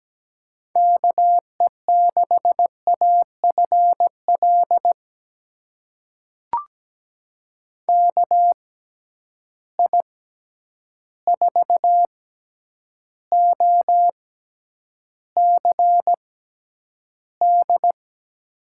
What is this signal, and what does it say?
Morse code, heard as a keyed tone, about 17 wpm, spaced as for 9 wpm, 700 Hz. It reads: KE6AFL KI4OCD
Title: USA Call Signs - 6 Characters - 6x Character Spacing - 17wpm